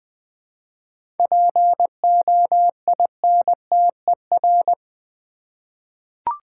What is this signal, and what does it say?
Morse code, heard as a keyed tone, about 20 wpm, 700 Hz. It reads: POINTER
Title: Easy Words with Hints - 20wpm